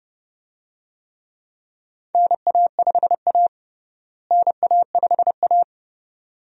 Morse code, heard as a keyed tone, about 31 wpm, 700 Hz. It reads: NA5A NA5A
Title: USA Call Signs - 4 Characters - Repeated 2x - 30wpm